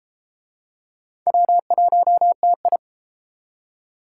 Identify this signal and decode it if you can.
Morse code, heard as a keyed tone, about 33 wpm, 700 Hz. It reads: W1TI